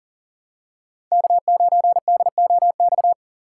Morse code, heard as a keyed tone, about 40 wpm, 700 Hz. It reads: K9DOX
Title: USA Call Signs - 5 Characters - 2x Character Spacing - 40wpm